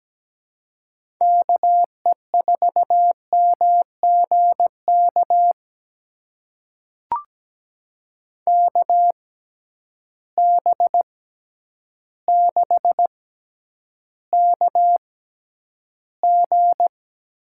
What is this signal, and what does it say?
Morse code, heard as a keyed tone, about 17 wpm, spaced as for 9 wpm, 700 Hz. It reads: KE4MGK KB6KG
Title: USA Call Signs - 6 Characters - 6x Character Spacing - 17wpm